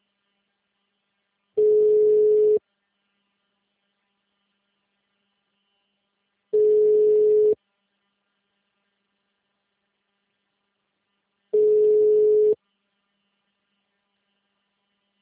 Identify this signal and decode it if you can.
Ringback tone (Europe, 425 Hz).
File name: out- -2018- - - .28421